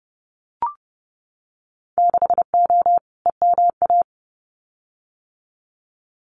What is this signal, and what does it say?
Morse code, heard as a keyed tone, about 30 wpm, 700 Hz. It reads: 6O EMA
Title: Winter Field Day Exchanges Encore 30wpm